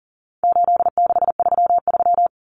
Morse code, heard as a keyed tone, about 40 wpm, 700 Hz. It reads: 8633
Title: Numbers - 4-Digit 40wpm